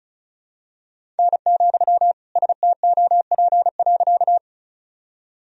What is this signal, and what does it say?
Morse code, heard as a keyed tone, about 36 wpm, 700 Hz. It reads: N, STOP.
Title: Sentences from Top 300 Words and Abbreviations 35wpm